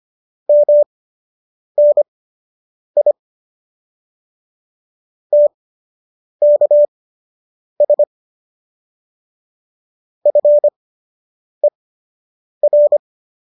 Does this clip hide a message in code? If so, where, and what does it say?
Yes: Morse code from 0.5 to 13.0 s, heard as a keyed tone, about 25 wpm, spaced as for 8 wpm, 590 Hz. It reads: MNI TKS FER